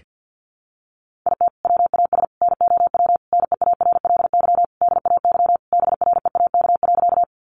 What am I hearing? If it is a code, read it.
Morse code, heard as a keyed tone, about 50 wpm, 700 Hz. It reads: IT WAS NOW NEARLY DAY BREAK.
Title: Sentences from Top 1300 Words 50wpm